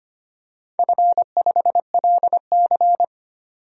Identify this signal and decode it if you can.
Morse code, heard as a keyed tone, about 25 wpm, 700 Hz. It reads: F5LC
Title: DXCC-100 Call Signs - 4 Characters - 25wpm